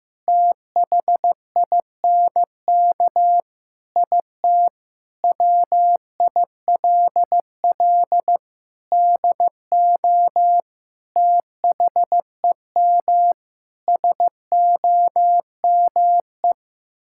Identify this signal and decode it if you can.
Morse code, heard as a keyed tone, about 15 wpm, 700 Hz. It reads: THINK IT WILL DO THEM SOME
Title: Sentences from Top 1000 Words - Review Part 6 - 15wpm